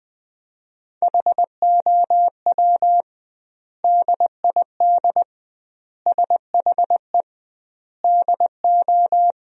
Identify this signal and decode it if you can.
Morse code, heard as a keyed tone, about 21 wpm, 700 Hz. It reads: HOW DID SHE DO
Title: 2x Word-Spacing - Sentences from Top 100 Words - 20wpm